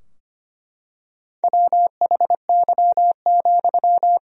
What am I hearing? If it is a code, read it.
Morse code, heard as a keyed tone, about 25 wpm, 700 Hz. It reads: WHY,
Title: 4x Word-Spacing - Sentences from Top 200 Words - 25wpm